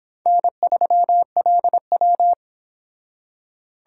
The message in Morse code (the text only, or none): N3LW